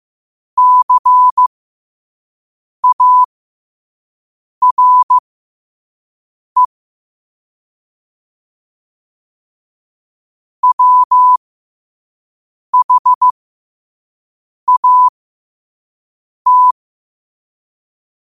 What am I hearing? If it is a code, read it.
Morse code, heard as a keyed tone, about 15 wpm, spaced as for 5 wpm, 1000 Hz. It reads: CARE WHAT
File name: qotdfiles / 5wpm